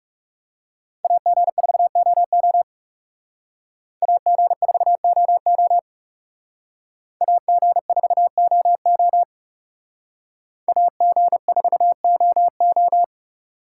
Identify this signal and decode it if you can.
Morse code, heard as a keyed tone, about 38 wpm, 700 Hz. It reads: AG4OO AG4OO AG4OO AG4OO